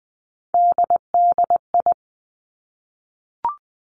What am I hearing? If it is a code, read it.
Morse code, heard as a keyed tone, about 20 wpm, 700 Hz. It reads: DDI
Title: Three Challenging Letter-Number Combinations 20wpm